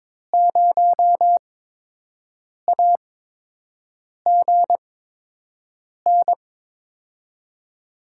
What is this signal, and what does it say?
Morse code, heard as a keyed tone, about 22 wpm, spaced as for 6 wpm, 700 Hz. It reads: 0AGN
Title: USA Call Signs - 8x Character Spacing - 22wpm